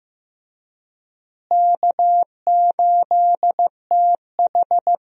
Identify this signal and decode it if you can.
Morse code, heard as a keyed tone, about 15 wpm, 700 Hz. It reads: K8TH
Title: USA Call Signs - 4 Characters - Repeated 3x - 15wpm